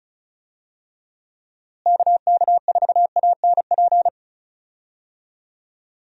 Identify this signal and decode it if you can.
Morse code, heard as a keyed tone, about 36 wpm, 700 Hz. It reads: KK4ANP
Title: USA Call Signs - 6 Characters - Repeated 2x - 35wpm